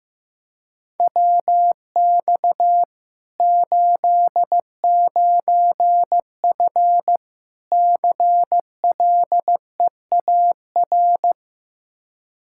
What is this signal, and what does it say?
Morse code, heard as a keyed tone, about 15 wpm, 700 Hz. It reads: WX 89F CLEAR
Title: Practice QSO - Exchange 2 - Info - 15wpm